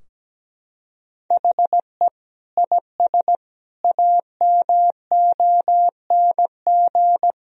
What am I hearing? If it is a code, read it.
Morse code, heard as a keyed tone, about 17 wpm, 700 Hz. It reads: HE IS AMONG
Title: Sentences from Top 1000 Words - Review Part 3 - 17wpm